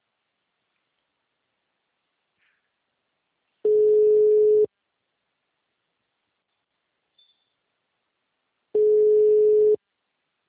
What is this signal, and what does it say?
Ringback tone (Europe, 425 Hz).